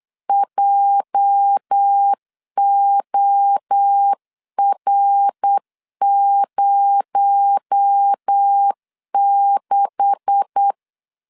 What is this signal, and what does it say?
Morse code, heard as a keyed tone, about 8 wpm, 800 Hz. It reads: JOR06
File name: LEZIONI / 9 286 8wpm 3425